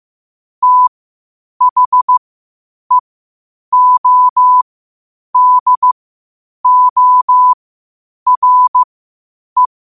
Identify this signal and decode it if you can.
Morse code, heard as a keyed tone, about 15 wpm, spaced as for 8 wpm, 1000 Hz. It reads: THEODORE
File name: qotdfiles / 8wpm